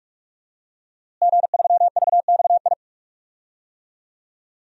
Morse code, heard as a keyed tone, about 44 wpm, 700 Hz. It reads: G3VXI
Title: DXCC-100 Call Signs - 5 Characters - Repeated 2x - 45wpm